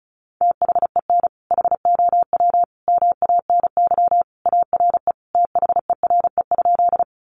Morse code, heard as a keyed tone, about 34 wpm, 700 Hz. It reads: THEN HOW MANY ARE THERE?
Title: Sentences from Top 100 Words 35wpm